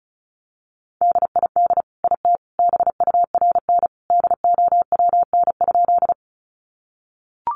Morse code, heard as a keyed tone, about 34 wpm, 700 Hz. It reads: DID IT BURN DOWN?
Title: Sentences from Top 1300 Words 35wpm